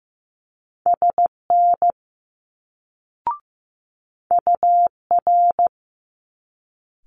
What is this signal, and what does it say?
Morse code, heard as a keyed tone, about 15 wpm, 700 Hz. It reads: SN UR
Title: Multi-letter Phonemes Quick Fire 15wpm